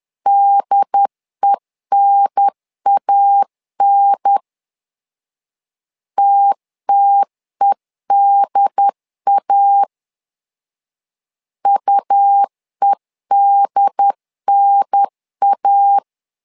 Morse code, heard as a keyed tone, about 10 wpm, 800 Hz. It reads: DENAN TTEDA UEDNA
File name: LEZIONI / 2 UHDN / 2 UHDN 10wpm 1904